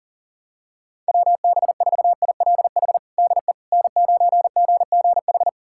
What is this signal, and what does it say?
Morse code, heard as a keyed tone, about 40 wpm, 700 Hz. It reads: WB4ILH DE N9GGH
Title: USA Callsign DE Callsign 40wpm